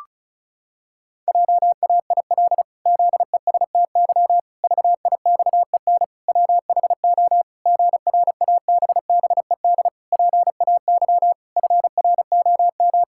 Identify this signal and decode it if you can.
Morse code, heard as a keyed tone, about 34 wpm, 700 Hz. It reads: JAIL ZESTY VIXEN WHO GRABBED PAY FROM
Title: Pangram Challenge v2 - 35wpm